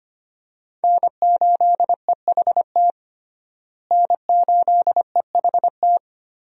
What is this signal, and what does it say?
Morse code, heard as a keyed tone, about 25 wpm, 700 Hz. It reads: N8EHT N8EHT
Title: USA Call Signs - 5 Characters - Repeated 3x - 25wpm